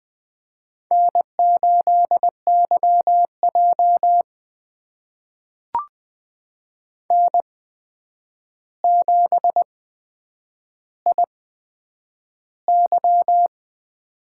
Morse code, heard as a keyed tone, about 20 wpm, spaced as for 9 wpm, 700 Hz. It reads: N8YJ N7IY
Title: USA Call Signs - 4 Characters - 8x Character Spacing - 20wpm